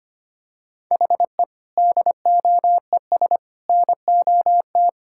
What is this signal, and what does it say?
Morse code, heard as a keyed tone, about 25 wpm, 700 Hz. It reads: HE DOES NOT
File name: practice / 25wpm / Sentences with 4-Word Phrases - 25wpm